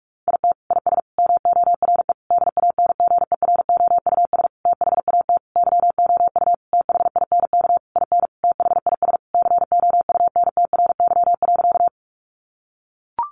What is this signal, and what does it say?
Morse code, heard as a keyed tone, about 44 wpm, 700 Hz. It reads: IT IS MORE DANGEROUS THAT YOU THINK IN THIS COUNTRY.
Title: Sentences from Top 1200 Words 45wpm